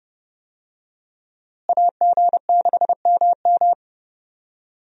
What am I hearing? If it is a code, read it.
Morse code, heard as a keyed tone, about 30 wpm, 700 Hz. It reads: AG6MM
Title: USA Call Signs - 5 Characters - Repeated 3x - 30wpm